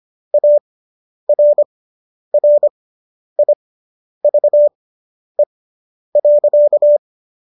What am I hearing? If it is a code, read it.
Morse code, heard as a keyed tone, about 25 wpm, spaced as for 10 wpm, 590 Hz. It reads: ARRIVE.